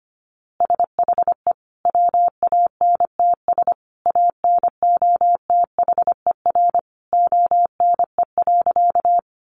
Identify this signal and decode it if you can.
Morse code, heard as a keyed tone, about 25 wpm, 700 Hz. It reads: SHE WANTS ANOTHER ONE.
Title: Easy Phrases 25wpm